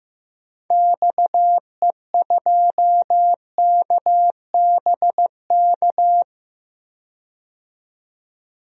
Morse code, heard as a keyed tone, about 15 wpm, 700 Hz. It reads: XE2KBK
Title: DXCC-100 Call Signs - 6 Characters - 15wpm